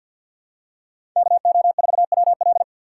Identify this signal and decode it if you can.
Morse code, heard as a keyed tone, about 50 wpm, 700 Hz. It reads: KO4WL
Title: USA Call Signs - 5 Characters - 2x Character Spacing - 50wpm